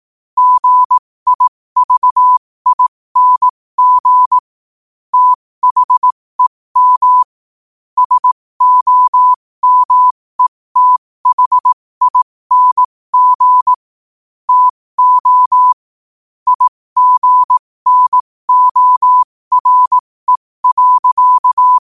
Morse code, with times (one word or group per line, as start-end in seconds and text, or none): 0.4-4.4 GIVING
5.1-7.2 THEM
8.0-13.7 SOMETHING
14.5-15.7 TO
16.5-21.8 IGNORE.